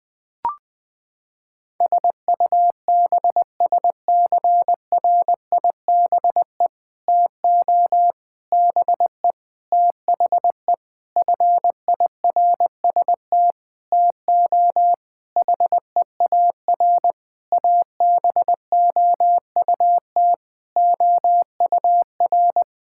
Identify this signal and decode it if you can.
Morse code, heard as a keyed tone, about 20 wpm, 700 Hz. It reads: SUBSCRIBE TO BE THE FIRST TO HEAR ABOUT OUR
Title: Sentences with 6-Word Phrases - 20wpm